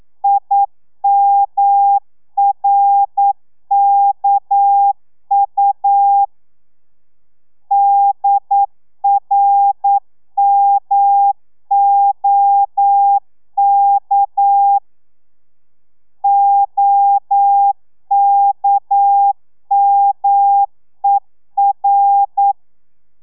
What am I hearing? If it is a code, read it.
Morse code, heard as a keyed tone, about 9 wpm, 800 Hz. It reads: IMRKU DRMOK OKMER
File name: telegrafia / i2tfj / Lez6 40